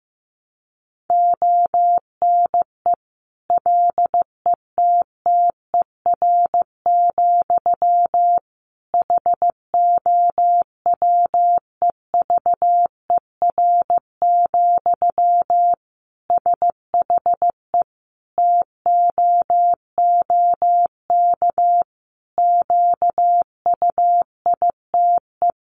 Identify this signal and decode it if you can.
Morse code, heard as a keyed tone, about 15 wpm, 700 Hz. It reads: ONE LETTER, HOWEVER, SHE TOOK QUITE
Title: Sentences from Top 1200 Words 15wpm